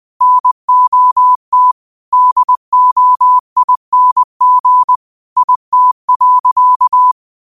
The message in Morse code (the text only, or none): NOT DOING IT.